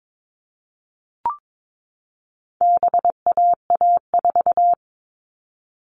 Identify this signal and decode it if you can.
Morse code, heard as a keyed tone, about 22 wpm, 700 Hz. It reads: BAA4
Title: All Cut Numbers - 4-Digit 22wpm